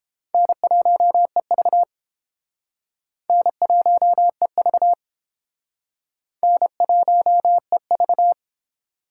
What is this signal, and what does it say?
Morse code, heard as a keyed tone, about 31 wpm, 700 Hz. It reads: N1EV N1EV N1EV